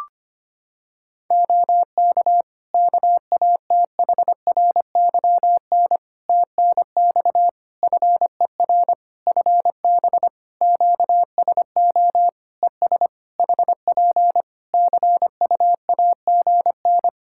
Morse code, heard as a keyed tone, about 25 wpm, 700 Hz. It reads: OK KATHRYN TNX FER FB QSO ES HP CUAGN
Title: K1USN SST QSO Practice 25wpm